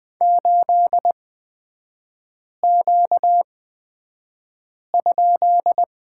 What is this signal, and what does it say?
Morse code, heard as a keyed tone, about 20 wpm, spaced as for 5 wpm, 700 Hz. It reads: 8Q?
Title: 236 Lesson 38e - Letter q - Three Characters 20wpm